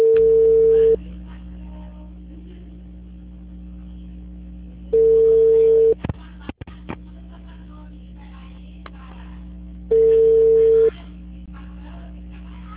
Ringback tone (China, 450 Hz).